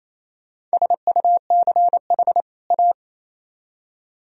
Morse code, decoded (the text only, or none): SUCH A